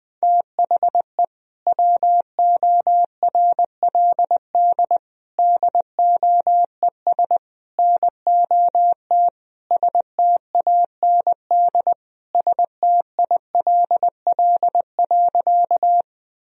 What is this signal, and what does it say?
Morse code, heard as a keyed tone, about 20 wpm, 700 Hz. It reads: THE WORLD DOES NOT STAND STILL.
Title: Sentences from Top 1000 Words - Review Part 1 - 20wpm